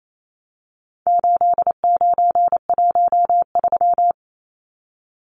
Morse code, heard as a keyed tone, about 28 wpm, 700 Hz. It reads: 8913
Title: Numbers - 4-Digit 28wpm